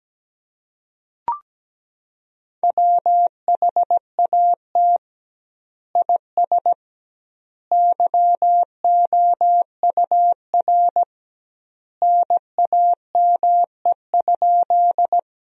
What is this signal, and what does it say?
Morse code, heard as a keyed tone, about 17 wpm, 700 Hz. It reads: WHAT IS YOUR NAME?
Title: 2x Word-Spacing - Sentences from Top 200 Words - 17wpm